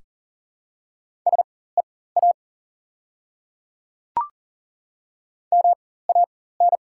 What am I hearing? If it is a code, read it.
Morse code, heard as a keyed tone, about 37 wpm, spaced as for 18 wpm, 700 Hz. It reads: SEA MAN
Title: 4x Character Spacing - Top 500 Words - 40wpm